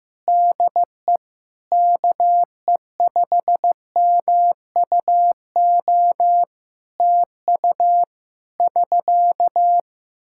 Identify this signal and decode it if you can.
Morse code, heard as a keyed tone, about 15 wpm, 700 Hz. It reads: DE KE5MUO TU <SK>
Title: Practice QSO - Exchange 4 - Ending - 15wpm